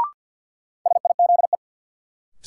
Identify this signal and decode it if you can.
Morse code, heard as a keyed tone, about 52 wpm, 700 Hz. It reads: SIZE
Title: Sets of 3 Words - New Word Drill - 50wpm